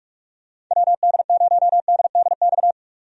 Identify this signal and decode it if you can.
Morse code, heard as a keyed tone, about 41 wpm, 700 Hz. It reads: WD0DDX
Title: USA Call Signs - 6 Characters - 45wpm